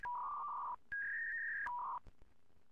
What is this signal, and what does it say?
Fax machine handshake.